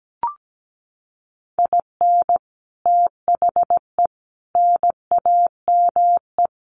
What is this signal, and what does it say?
Morse code, heard as a keyed tone, about 17 wpm, 700 Hz. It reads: IN THE NAME
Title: Sets of 4 Words 17wpm